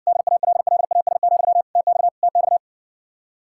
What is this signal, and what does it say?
Morse code, heard as a keyed tone, about 60 wpm, 700 Hz. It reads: DALLAS, TX TX